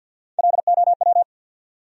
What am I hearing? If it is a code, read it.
Morse code, heard as a keyed tone, about 65 wpm, spaced as for 57 wpm, 700 Hz. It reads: LOW
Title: Speed-Racing Top 500 Words 50wpm